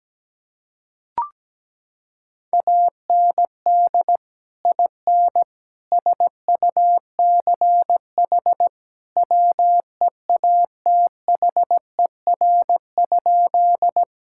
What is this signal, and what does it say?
Morse code, heard as a keyed tone, about 17 wpm, 700 Hz. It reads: AND IN SUCH WEATHER?
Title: Sentences from Top 1000 Words - Review Part 5 - 17wpm